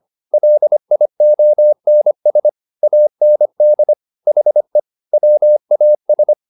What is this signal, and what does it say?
Morse code, heard as a keyed tone, about 25 wpm, 600 Hz. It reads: LIONS AND HE WAS